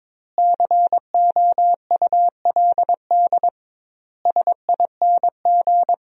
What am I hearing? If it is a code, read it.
Morse code, heard as a keyed tone, about 22 wpm, 700 Hz. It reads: COULD SING
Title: 2x Word-Spacing - Sentences from Top 500 Words - 22wpm